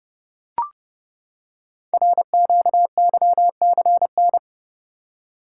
Morse code, heard as a keyed tone, about 28 wpm, 700 Hz. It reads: RQYCN
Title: Code Groups - 5 Letters - 30wpm